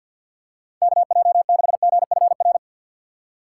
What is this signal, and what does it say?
Morse code, heard as a keyed tone, about 48 wpm, 700 Hz. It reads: KJ6GFR
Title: Speed Racing - USA 2x3 Call Signs 50wpm